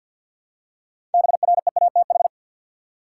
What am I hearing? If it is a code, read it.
Morse code, heard as a keyed tone, about 50 wpm, 700 Hz. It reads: BREATH
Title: Six Letter Words - 50wpm